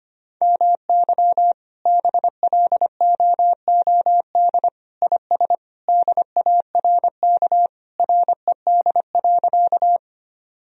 Morse code, heard as a keyed tone, about 25 wpm, 700 Hz. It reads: MY BLOOD IS DARK RED.
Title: Sentences from Top 1000 Words - Review Part 2 - 25wpm